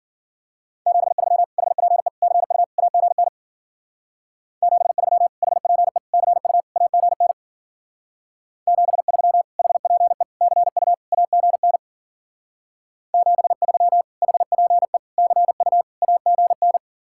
Morse code, heard as a keyed tone, about 51 wpm, 700 Hz. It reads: 73 HPE CU AGN 73 HPE CU AGN 73 HPE CU AGN 73 HPE CU AGN